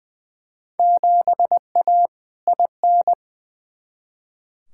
Morse code, heard as a keyed tone, about 20 wpm, 700 Hz. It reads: 7A IN
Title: Field Day Exchanges Encore - 20wpm